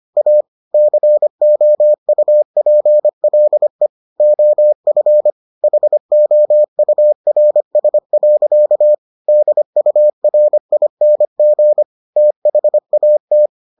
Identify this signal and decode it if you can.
Morse code, heard as a keyed tone, about 25 wpm, 600 Hz. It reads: A COUPLE OF HOURS. DURING THAT